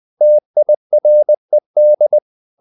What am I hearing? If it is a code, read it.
Morse code, heard as a keyed tone, about 20 wpm, 600 Hz. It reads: TIRED
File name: the old man and the sea / the old man and the sea part 1 20wpm